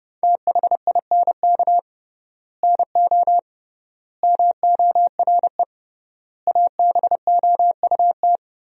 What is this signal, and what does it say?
Morse code, heard as a keyed tone, about 30 wpm, 700 Hz. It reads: THINK NO MORE ABOUT
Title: 3x Word-Spacing - Sentences from Top 200 Words - 30wpm